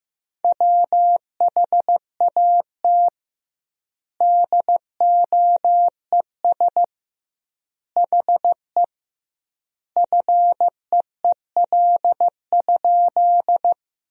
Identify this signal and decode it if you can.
Morse code, heard as a keyed tone, about 15 wpm, 700 Hz. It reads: WHAT DOES HE FEEL?
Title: 2x Word-Spacing - Sentences from Top 400 Words - 15wpm